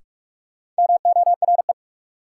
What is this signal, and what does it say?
Morse code, heard as a keyed tone, about 47 wpm, 700 Hz. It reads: MORE
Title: Sentences from Top 100 Words - New Word Drill - 45wpm